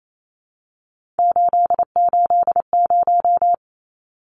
Morse code, heard as a keyed tone, about 28 wpm, 700 Hz. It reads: 880
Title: Numbers - 3-Digit 28wpm